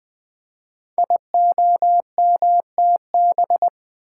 Morse code, heard as a keyed tone, about 20 wpm, 700 Hz. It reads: IOMTB
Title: Code Groups - 5 Letters - 20wpm